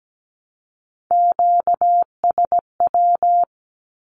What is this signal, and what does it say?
Morse code, heard as a keyed tone, about 17 wpm, 700 Hz. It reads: QSW
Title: Q-Codes with Spoken Meaning - 17wpm